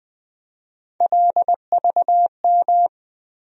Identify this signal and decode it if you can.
Morse code, heard as a keyed tone, about 20 wpm, 700 Hz. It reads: LVM
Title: USA Call Sign Postfixes - 3 Letters 20wpm